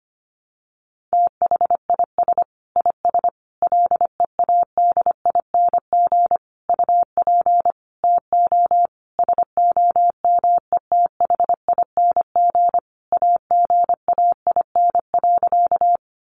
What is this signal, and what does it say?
Morse code, heard as a keyed tone, about 25 wpm, 700 Hz. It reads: THIS IS LEADING UP TO SOMETHING AGAIN.
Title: Sentences from Top 1100 Words 25wpm